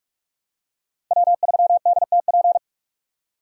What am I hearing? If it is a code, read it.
Morse code, heard as a keyed tone, about 43 wpm, 700 Hz. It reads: W3DTP
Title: USA Call Signs - 5 Characters - 45wpm